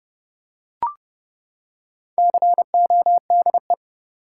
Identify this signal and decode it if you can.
Morse code, heard as a keyed tone, about 30 wpm, 700 Hz. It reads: CODE
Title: Four Letter Words - 30wpm